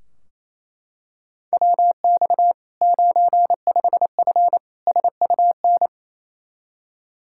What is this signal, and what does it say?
Morse code, heard as a keyed tone, about 27 wpm, 700 Hz. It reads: WX 95F SUN
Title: Practice QSO - Exchange 2 - Info - 28wpm